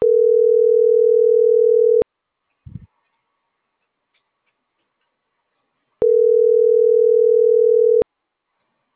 Ringback tone (North America), 440 + 480 Hz.
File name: CENTRAL NAVIERAS / out juan.salcedo